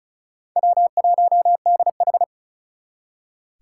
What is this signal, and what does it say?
Morse code, heard as a keyed tone, about 47 wpm, spaced as for 41 wpm, 700 Hz. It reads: W1DH
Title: Speed Racing - USA 1x2 Call Signs 35wpm